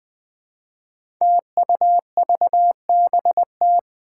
Morse code, heard as a keyed tone, about 21 wpm, 700 Hz. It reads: TUVBT
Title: Five Challenging Letter-Number Combinations 20wpm